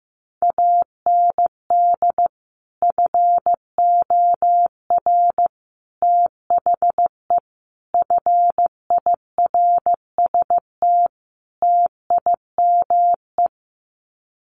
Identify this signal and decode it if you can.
Morse code, heard as a keyed tone, about 15 wpm, 700 Hz. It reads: AND FOR THE FIRST TIME
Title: Sets of 5 Words 15wpm